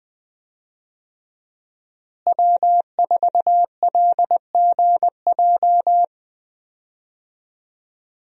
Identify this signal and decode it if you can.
Morse code, heard as a keyed tone, about 20 wpm, 700 Hz. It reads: W4LGJ
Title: USA Call Signs - 5 Characters - Repeated 2x - 20wpm